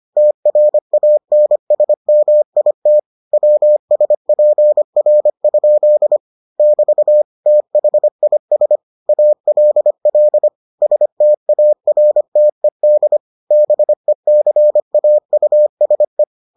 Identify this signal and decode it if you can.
Morse code, heard as a keyed tone, about 25 wpm, 600 Hz. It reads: TRANSMIT WSPR? <BT> THIS ALL STARTED BECAUSE